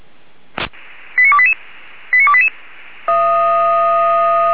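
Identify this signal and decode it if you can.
Five-tone selective call (ZVEI), heard as quick runs of five tones, tones of about 70 ms. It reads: 89288 89288
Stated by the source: ZVEI five-tone selective call 89305 received by SDR, fire brigade Orlamünde (feuerwehr-orlamuende).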